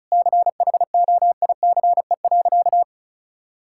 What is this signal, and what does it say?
Morse code, heard as a keyed tone, about 36 wpm, 700 Hz. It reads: CHOICE.
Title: Sentences from Top 1000 Words - Review Part 6 - 35wpm